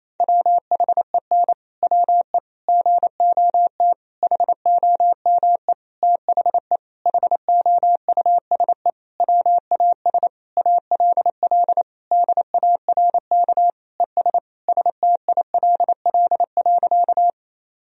Morse code, heard as a keyed tone, about 28 wpm, 700 Hz. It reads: WHEN WE GOT HOME THE HOUSE WAS ALL DARK ES STILL.
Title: Sentences from Top 500 Words and Abbreviations 28wpm